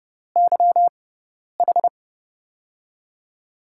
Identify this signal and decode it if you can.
Morse code, heard as a keyed tone, about 30 wpm, spaced as for 10 wpm, 700 Hz. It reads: YH
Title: USA Call Signs - 6 Characters - 6x Character Spacing - 30wpm